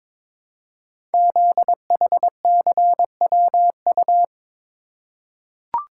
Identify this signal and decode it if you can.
Morse code, heard as a keyed tone, about 22 wpm, 700 Hz. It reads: ZHCWU
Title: Code Groups - 5 Letters - 22wpm